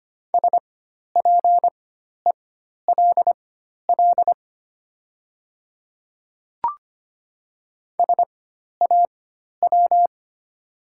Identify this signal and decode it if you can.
Morse code, heard as a keyed tone, about 25 wpm, spaced as for 12 wpm, 700 Hz. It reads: SPELL SAW
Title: 4x Character Spacing - Top 500 Words - 25wpm